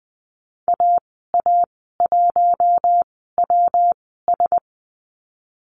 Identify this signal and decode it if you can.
Morse code, heard as a keyed tone, about 20 wpm, spaced as for 15 wpm, 700 Hz. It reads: AA1WS
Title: 173 Lesson 28g - Number 9 - Call Signs 20wpm